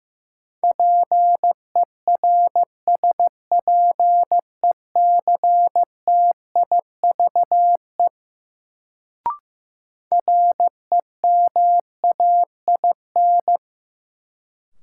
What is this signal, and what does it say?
Morse code, heard as a keyed tone, about 15 wpm, 700 Hz. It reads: PERSPECTIVE REMAIN
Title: Sentences from Top 1200 Words - New Word Drill - 15wpm